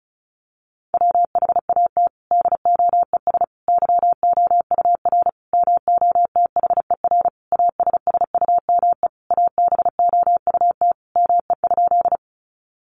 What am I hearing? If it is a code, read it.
Morse code, heard as a keyed tone, about 34 wpm, 700 Hz. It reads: WHAT DOES YOUR MOTHER ASSUME ABOUT ME?
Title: Sentences from Top 1500 Words 35wpm